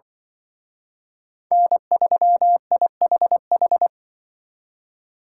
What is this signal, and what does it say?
Morse code, heard as a keyed tone, about 25 wpm, 700 Hz. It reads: N3IHH
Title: Speed Racing - USA 1x3 Call Signs 20wpm